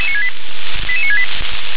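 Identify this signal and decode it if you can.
Five-tone selective call (ZVEI), heard as quick runs of five tones, tones of about 70 ms. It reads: 88968
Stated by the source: ZVEI five-tone selective call, code 88973, received with an SDR.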